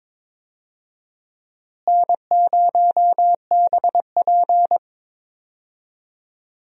Morse code, heard as a keyed tone, about 22 wpm, 700 Hz. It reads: N0BP
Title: USA Call Signs - 4 Characters - Repeated 3x - 22wpm